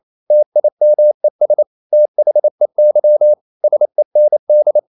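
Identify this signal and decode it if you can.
Morse code, heard as a keyed tone, about 28 wpm, 600 Hz. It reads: TIMES THEY SEND
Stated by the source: Morse code practice at 28 wpm, text of Ken Follett's novel Never in English, from CW med Gnister.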